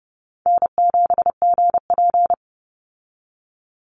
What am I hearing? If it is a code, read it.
Morse code, heard as a keyed tone, about 30 wpm, 700 Hz. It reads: N7GP
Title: USA Call Signs 30wpm